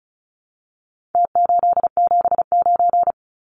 Morse code, heard as a keyed tone, about 34 wpm, 700 Hz. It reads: T879
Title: Numbers - Contest Style - 4-Digit 35wpm